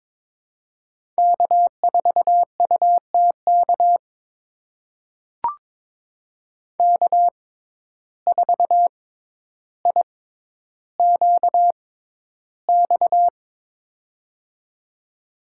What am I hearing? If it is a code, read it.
Morse code, heard as a keyed tone, about 22 wpm, spaced as for 11 wpm, 700 Hz. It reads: K4UTK K4IQX
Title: USA Call Signs - 5 Characters - 6x Character Spacing - 22wpm